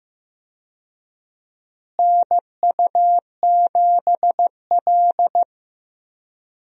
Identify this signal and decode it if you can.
Morse code, heard as a keyed tone, about 15 wpm, 700 Hz. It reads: NU7L
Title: USA Call Signs - 4 Characters - Repeated 3x - 15wpm